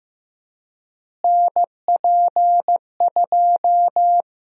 Morse code, heard as a keyed tone, about 15 wpm, 700 Hz. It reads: NP2
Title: USA Call Sign Prefixes - Repeated 2x - 15wpm